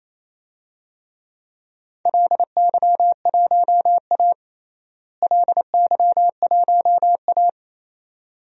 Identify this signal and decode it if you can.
Morse code, heard as a keyed tone, about 28 wpm, 700 Hz. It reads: LY1A LY1A
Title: DXCC-100 Call Signs - 4 Characters - Repeated 2x - 28wpm